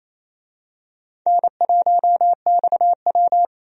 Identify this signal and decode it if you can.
Morse code, heard as a keyed tone, about 28 wpm, 700 Hz. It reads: N1XW
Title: USA Call Signs - 4 Characters - 28wpm